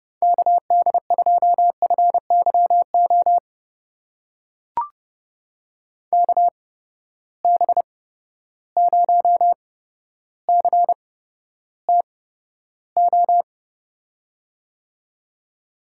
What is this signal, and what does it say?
Morse code, heard as a keyed tone, about 31 wpm, spaced as for 13 wpm, 700 Hz. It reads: KD2FYO KB0CTO
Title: USA Call Signs - 6 Characters - 8x Character Spacing - 30wpm